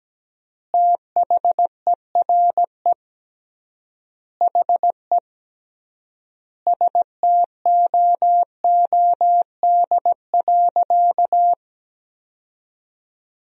Morse code, heard as a keyed tone, about 17 wpm, 700 Hz. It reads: THERE HE STOOD.
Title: 3x Word-Spacing - Sentences from Top 500 Words - 17wpm